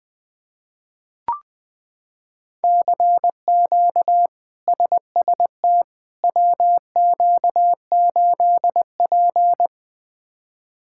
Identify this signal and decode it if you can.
Morse code, heard as a keyed tone, about 20 wpm, 700 Hz. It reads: CQ SST WQ8P
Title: K1USN SST QSO Practice - 20wpm